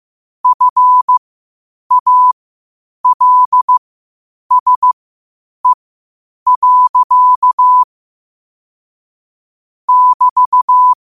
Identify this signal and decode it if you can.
Morse code, heard as a keyed tone, about 15 wpm, spaced as for 8 wpm, 1000 Hz. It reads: FALSE. <BT>
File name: qotdfiles / 8wpm